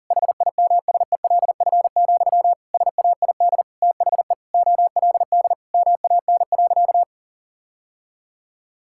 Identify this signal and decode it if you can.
Morse code, heard as a keyed tone, about 40 wpm, 700 Hz. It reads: HIMSELF, SAID THE OLD MAN.
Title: Sentences from Top 1600 Words 40wpm